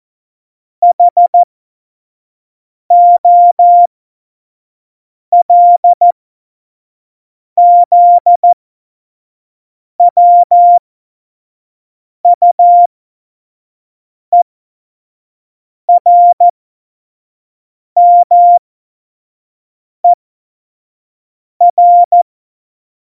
Morse code, heard as a keyed tone, about 14 wpm, spaced as for 5 wpm, 700 Hz. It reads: HOLZWUERMER